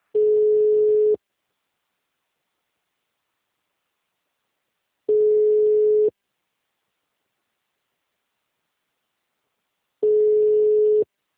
Ringback tone (Europe, 425 Hz).
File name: out- -2001- - - .35355